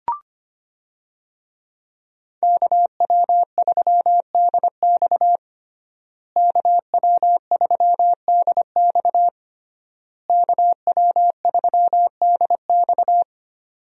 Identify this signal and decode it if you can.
Morse code, heard as a keyed tone, about 25 wpm, 700 Hz. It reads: KW3DX KW3DX KW3DX
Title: USA Call Signs - 5 Characters - Repeated 3x - 25wpm